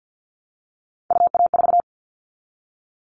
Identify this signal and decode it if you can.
Morse code, heard as a keyed tone, about 50 wpm, 700 Hz. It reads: UA4
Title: Callsign Prefixes for Top 25 DXCC - Repeated 3x - 50wpm